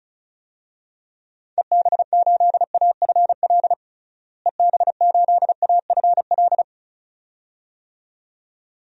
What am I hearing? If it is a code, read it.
Morse code, heard as a keyed tone, about 36 wpm, 700 Hz. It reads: EB8AFL EB8AFL
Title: DXCC-100 Call Signs - 6 Characters - Repeated 2x - 35wpm